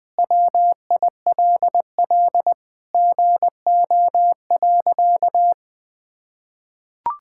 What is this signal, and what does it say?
Morse code, heard as a keyed tone, about 20 wpm, 700 Hz. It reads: WILL GO.
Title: Sentences with 4-Word Phrases - 20wpm